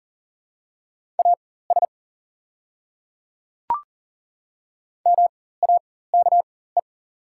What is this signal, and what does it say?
Morse code, heard as a keyed tone, about 37 wpm, spaced as for 18 wpm, 690 Hz. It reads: AS MAKE
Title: 4x Character Spacing - Top 100 Words - 40wpm